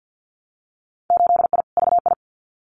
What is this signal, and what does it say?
Morse code, heard as a keyed tone, about 50 wpm, 700 Hz. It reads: 8I VI
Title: Winter Field Day Exchanges Encore 50wpm